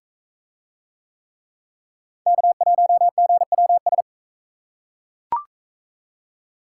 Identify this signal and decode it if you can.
Morse code, heard as a keyed tone, about 44 wpm, 700 Hz. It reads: K1GWS